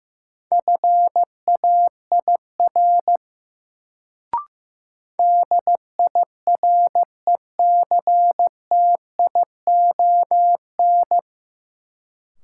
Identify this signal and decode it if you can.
Morse code, heard as a keyed tone, about 15 wpm, 700 Hz. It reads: FAIR DIRECTION
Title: Sentences from Top 1000 Words - Review Part 5 - New Word Drill - 15wpm